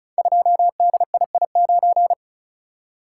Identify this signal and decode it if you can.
Morse code, heard as a keyed tone, about 36 wpm, 700 Hz. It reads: 2DII9
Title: Code Groups - 5 Letter:Numbers - 35wpm